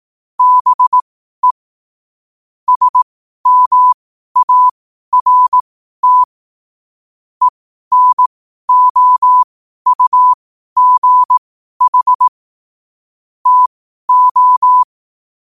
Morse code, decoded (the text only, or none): BE SMART ENOUGH TO